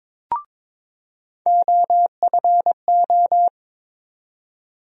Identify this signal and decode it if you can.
Morse code, heard as a keyed tone, about 23 wpm, 700 Hz. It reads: OFO
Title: USA Call Sign Postfixes - 3 Letters 22wpm